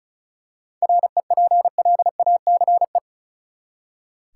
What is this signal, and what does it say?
Morse code, heard as a keyed tone, about 34 wpm, 700 Hz. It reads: REPLACE
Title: Seven Letter Words - 35wpm